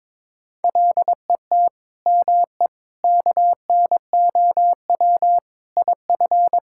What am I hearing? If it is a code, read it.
Morse code, heard as a keyed tone, about 22 wpm, 700 Hz. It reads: LET ME KNOW IF
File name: practice / 22wpm / Sets of 7 Words - 22wpm